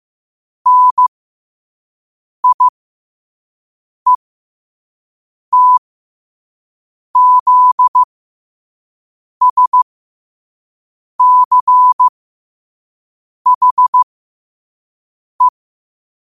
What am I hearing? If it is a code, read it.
Morse code, heard as a keyed tone, about 15 wpm, spaced as for 5 wpm, 1000 Hz. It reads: NIETZSCHE
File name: qotdfiles / 5wpm